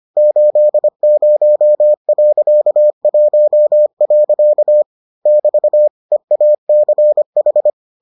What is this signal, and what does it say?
Morse code, heard as a keyed tone, about 25 wpm, 600 Hz. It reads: 80.1. <BT> EACH